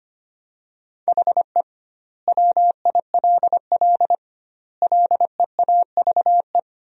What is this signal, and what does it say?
Morse code, heard as a keyed tone, about 25 wpm, 700 Hz. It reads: HE WILL LEAVE
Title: 2x Word-Spacing - Sentences from Top 400 Words - 25wpm